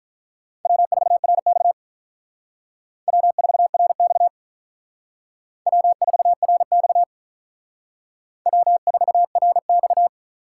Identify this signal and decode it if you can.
Morse code, heard as a keyed tone, about 42 wpm, 700 Hz. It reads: W4RX W4RX W4RX W4RX
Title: Speed Racing - USA 1x2 Call Signs 35wpm